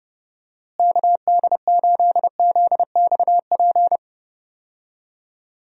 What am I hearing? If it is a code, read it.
Morse code, heard as a keyed tone, about 29 wpm, 700 Hz. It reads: KD8ZXP
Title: USA Call Signs - 6 Characters - Repeated 3x - 30wpm